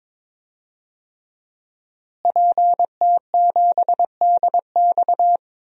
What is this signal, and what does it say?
Morse code, heard as a keyed tone, about 22 wpm, 700 Hz. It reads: PT7DX
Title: DXCC-100 Call Signs - 5 Characters - Repeated 3x - 22wpm